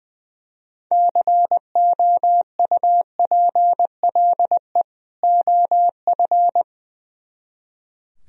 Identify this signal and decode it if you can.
Morse code, heard as a keyed tone, about 19 wpm, 700 Hz. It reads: COUPLE OF
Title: Speed-Racing - Sets of 2 Words 20wpm